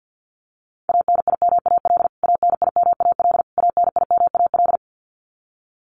Morse code, heard as a keyed tone, about 50 wpm, 700 Hz. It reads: ANIMAL ANIMAL ANIMAL
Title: Top 200 Words Repeated 50wpm